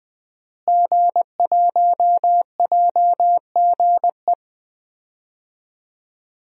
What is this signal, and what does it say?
Morse code, heard as a keyed tone, about 21 wpm, 700 Hz. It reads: G1JGE
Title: DXCC-100 Call Signs - 5 Characters - Repeated 3x - 20wpm